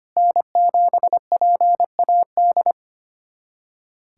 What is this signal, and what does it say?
Morse code, heard as a keyed tone, about 25 wpm, 700 Hz. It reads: N7PAD
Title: USA Callsign DE Callsign - 5 Characters - 25wpm